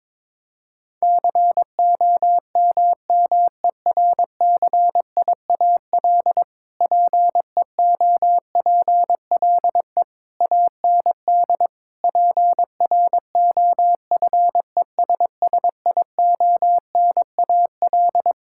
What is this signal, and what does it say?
Morse code, heard as a keyed tone, about 22 wpm, 700 Hz. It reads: COMMERCIAL PEOPLE AND PROFESSIONAL
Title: Sentences from Top 1000 Words - Review Part 5 - 22wpm